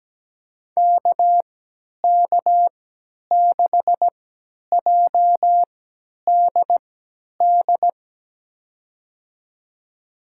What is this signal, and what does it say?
Morse code, heard as a keyed tone, about 17 wpm, spaced as for 10 wpm, 700 Hz. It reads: KK6JDD
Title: USA Call Signs - 6 Characters - 3x Character Spacing - 17wpm